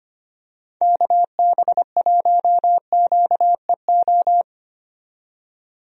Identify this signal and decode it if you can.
Morse code, heard as a keyed tone, about 25 wpm, 700 Hz. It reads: KB1QEO
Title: USA Call Signs - 6 Characters - 25wpm